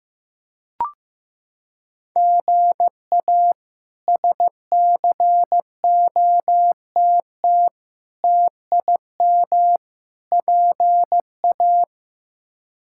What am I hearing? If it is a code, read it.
Morse code, heard as a keyed tone, about 15 wpm, 700 Hz. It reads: GA SCOTT TIM PA
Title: K1USN SST QSO Practice - 15wpm